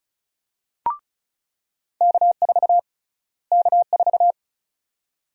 Morse code, heard as a keyed tone, about 34 wpm, 700 Hz. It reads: K4 K4
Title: USA Call Sign Prefixes - Repeated 2x - 35wpm